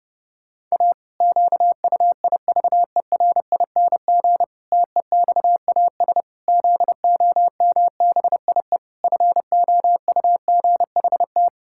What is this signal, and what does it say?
Morse code, heard as a keyed tone, about 32 wpm, 700 Hz. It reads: A QUIVERING TEXAS ZOMBIE FOUGHT
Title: Pangram Challenge v2 - 30wpm